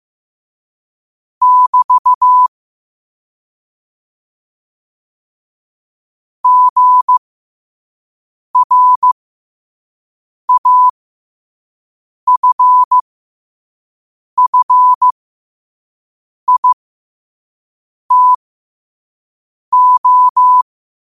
Morse code, heard as a keyed tone, about 15 wpm, spaced as for 5 wpm, 1000 Hz. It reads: <BT> GRAFFITO